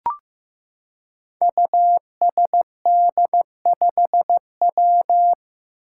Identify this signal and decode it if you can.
Morse code, heard as a keyed tone, about 15 wpm, 700 Hz. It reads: USD5W
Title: Five Challenging Letter-Number Combinations 15wpm